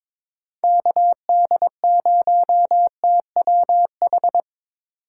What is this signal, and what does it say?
Morse code, heard as a keyed tone, about 25 wpm, 700 Hz. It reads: KD0TWH
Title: Speed Racing - USA 2x3 Call Signs 22wpm